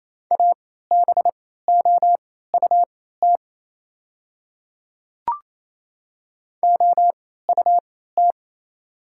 Morse code, heard as a keyed tone, about 28 wpm, spaced as for 16 wpm, 690 Hz. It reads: ABOUT OUT
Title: 3x Character Spacing - Top 100 Words - 28wpm